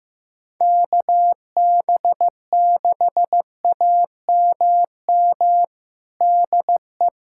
Morse code, heard as a keyed tone, about 16 wpm, 700 Hz. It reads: KB6AMM DE
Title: USA Callsign DE Callsign - 6 Characters - 15wpm